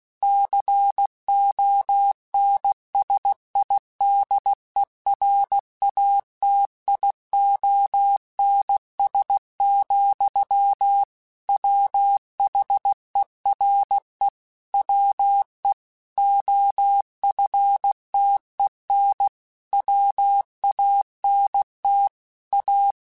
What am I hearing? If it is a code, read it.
Morse code, heard as a keyed tone, about 16 wpm, 800 Hz. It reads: CONSIDERATIONS, WHERE WE OFTEN WANT A
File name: Technical Subjects / Technical Subjects-mp3s / Sw Pwr Sup @15